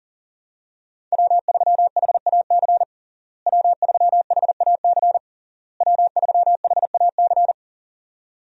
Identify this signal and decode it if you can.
Morse code, heard as a keyed tone, about 37 wpm, 700 Hz. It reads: W3HAC W3HAC W3HAC
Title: USA Call Signs - 5 Characters - Repeated 3x - 40wpm